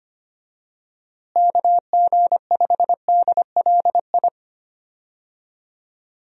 Morse code, heard as a keyed tone, about 25 wpm, 700 Hz. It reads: KG5DLI
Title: USA Call Signs - 6 Characters - Repeated 3x - 25wpm